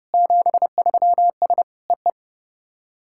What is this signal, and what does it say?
Morse code, heard as a keyed tone, about 32 wpm, 700 Hz. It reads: 73S EE
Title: SOTA QSO Practice 30wpm